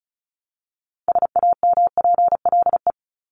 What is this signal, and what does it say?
Morse code, heard as a keyed tone, about 34 wpm, 700 Hz. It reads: SAMPLE
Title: Top 1501-1750 Words 35wpm